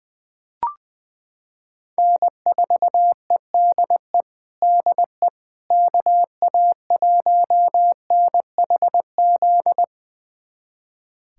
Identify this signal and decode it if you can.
Morse code, heard as a keyed tone, about 21 wpm, 700 Hz. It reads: N4EDE DE KA1NHZ
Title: USA Callsign DE Callsign 20wpm